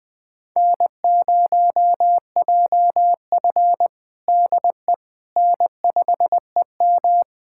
Morse code, heard as a keyed tone, about 20 wpm, 700 Hz. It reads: N0JF DE N5EM
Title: USA Callsign DE Callsign - 4 Characters - 20wpm